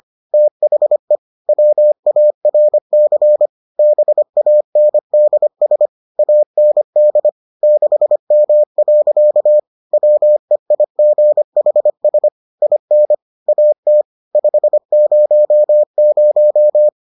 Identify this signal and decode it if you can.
Morse code, heard as a keyed tone, about 25 wpm, 600 Hz. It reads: THE WARC BANDS AND 6M. WEIGHS IN AT 500